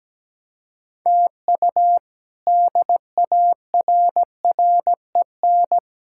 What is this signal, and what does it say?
Morse code, heard as a keyed tone, about 17 wpm, 700 Hz. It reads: TU DARREN
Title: K1USN SST QSO Practice - 17wpm